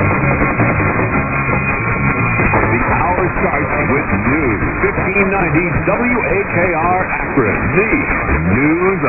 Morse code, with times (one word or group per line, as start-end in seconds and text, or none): none